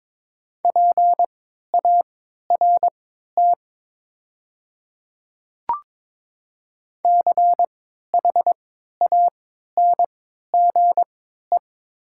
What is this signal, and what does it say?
Morse code, heard as a keyed tone, about 22 wpm, spaced as for 12 wpm, 700 Hz. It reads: PART CHANGE
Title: 3x Character Spacing - Top 200 Words - 22wpm